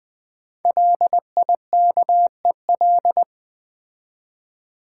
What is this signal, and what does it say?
Morse code, heard as a keyed tone, about 20 wpm, 700 Hz. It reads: LIKEL
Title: Build A Word - 5 Letters At A Time - 10-Letter Words - 20wpm